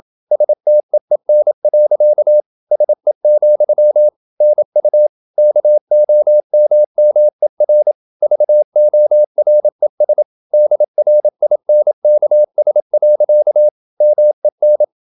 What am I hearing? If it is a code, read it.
Morse code, heard as a keyed tone, about 28 wpm, 600 Hz. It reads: STEEN. SE, NU KOMMER VORES DRINKS. MEN
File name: Ferie QRQ 27wpm